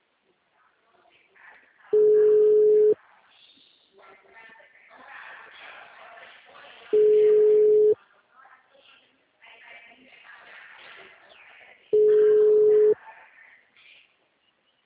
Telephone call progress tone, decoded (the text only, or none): ringback tone